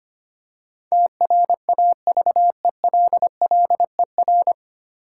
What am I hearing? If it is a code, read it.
Morse code, heard as a keyed tone, about 25 wpm, 700 Hz. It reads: TRAVELLER
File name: practice / 25wpm / 9 Letter Words - First 3 Letters Given - 25wpm